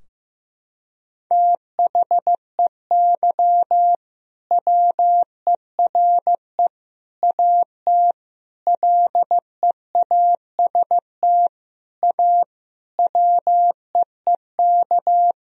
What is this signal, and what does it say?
Morse code, heard as a keyed tone, about 15 wpm, 700 Hz. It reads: THEY WERE AT LEAST A WEEK
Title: Sentences from Top 1000 Words - Review Part 2 - 15wpm